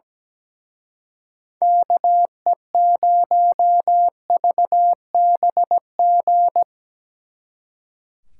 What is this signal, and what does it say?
Morse code, heard as a keyed tone, about 18 wpm, 700 Hz. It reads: KE0VBG
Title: Speed Racing - USA Call Signs 17wpm